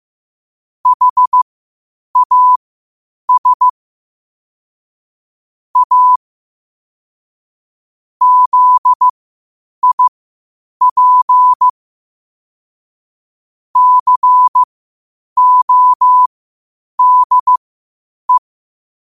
Morse code, heard as a keyed tone, about 15 wpm, spaced as for 8 wpm, 1000 Hz. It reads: HAS A ZIP CODE
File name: qotdfiles / 8wpm